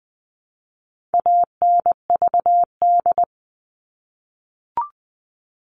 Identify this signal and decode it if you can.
Morse code, heard as a keyed tone, about 20 wpm, 700 Hz. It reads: ANVD
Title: All Cut Numbers - 4-Digit 20wpm